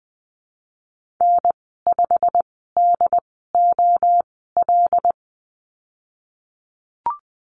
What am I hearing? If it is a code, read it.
Morse code, heard as a keyed tone, about 20 wpm, spaced as for 15 wpm, 700 Hz. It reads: N5DOL
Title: 088 Lesson 15g - Number 5 - Call Signs 20wpm